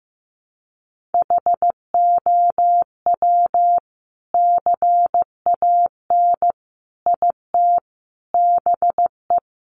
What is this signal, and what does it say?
Morse code, heard as a keyed tone, about 15 wpm, 700 Hz. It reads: HOW CAN IT BE
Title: Sentences from Top 1400 Words 15wpm